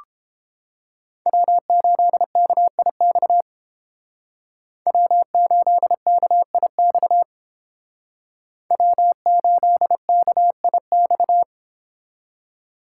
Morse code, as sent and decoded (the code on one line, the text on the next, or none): .-- ---.. -.- .. -..- / .-- ---.. -.- .. -..- / .-- ---.. -.- .. -..-
W8KIX W8KIX W8KIX